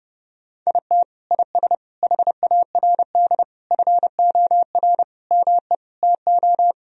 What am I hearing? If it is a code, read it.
Morse code, heard as a keyed tone, about 30 wpm, 700 Hz. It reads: IT IS HARD FOR ME TO
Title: Sentences from Top 1000 Words - Review Part 3 - 30wpm